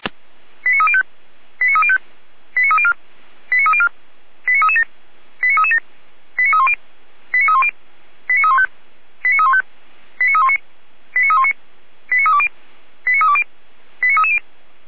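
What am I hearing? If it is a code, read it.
Five-tone selective call (ZVEI), heard as quick runs of five tones, tones of about 80 ms. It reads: 89305 89305 89304 89304 89337 89337 89310 89310 89315 89315 89319 89319 89320 89320 89330
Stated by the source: ZVEI five-tone selective call received by SDR, from audio.f85360.feuerwehr-orlamuende.org fire brigade server.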